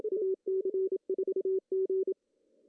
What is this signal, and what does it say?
Morse code, heard as a keyed tone, about 27 wpm, 380 Hz. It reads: AC4G
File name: CW / cqww-cw-k5tr